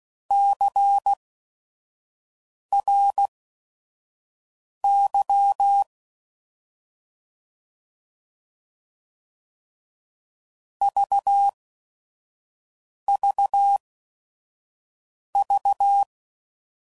Morse code, heard as a keyed tone, about 16 wpm, spaced as for 5 wpm, 800 Hz. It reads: CRY VVV
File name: Prefixes / By First Letter / Cs